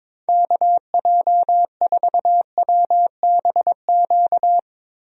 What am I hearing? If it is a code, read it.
Morse code, heard as a keyed tone, about 22 wpm, 700 Hz. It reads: KJ4WBQ
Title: USA Call Signs - 6 Characters - Repeated 2x - 22wpm